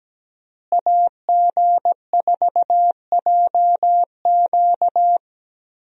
Morse code, heard as a keyed tone, about 17 wpm, 700 Hz. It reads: AG4JQ
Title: USA Call Signs - 5 Characters - Repeated 2x - 17wpm